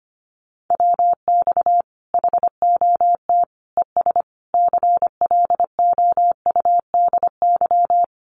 Morse code, heard as a keyed tone, about 25 wpm, 700 Hz. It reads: WX HOT ES CLOUDY
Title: Common QSO Phrases 25wpm